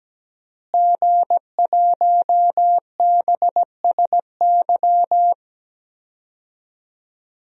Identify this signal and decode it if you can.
Morse code, heard as a keyed tone, about 17 wpm, 700 Hz. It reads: G1BSY
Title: DXCC-100 Call Signs - 5 Characters - Repeated 3x - 17wpm